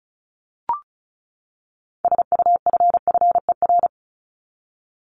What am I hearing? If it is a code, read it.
Morse code, heard as a keyed tone, about 34 wpm, 700 Hz. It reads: SUFFER
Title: Top 1501-1750 Words 35wpm